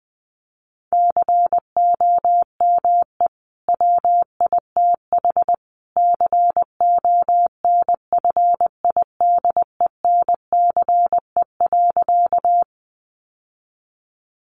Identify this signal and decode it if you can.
Morse code, heard as a keyed tone, about 20 wpm, 700 Hz. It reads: COME WITH CONFIDENCE.
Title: Sentences from Top 1500 Words 20wpm